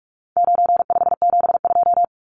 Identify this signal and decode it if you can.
Morse code, heard as a keyed tone, about 44 wpm, 700 Hz. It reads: 9572
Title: Numbers - 4-Digit 45wpm